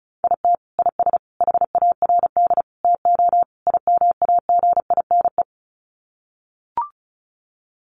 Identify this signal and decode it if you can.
Morse code, heard as a keyed tone, about 34 wpm, 700 Hz. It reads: IT IS HARD TO IMAGINE
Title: Sets of 5 Words Encore2 35wpm